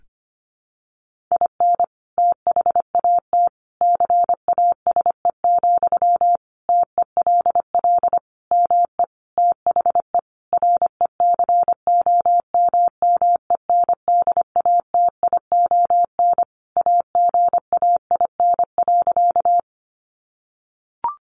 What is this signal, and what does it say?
Morse code, heard as a keyed tone, about 25 wpm, 700 Hz. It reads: IN THAT CASE, TELL ME THE RECOMMENDATION AGAIN.
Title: Sentences from Top 1400 Words 25wpm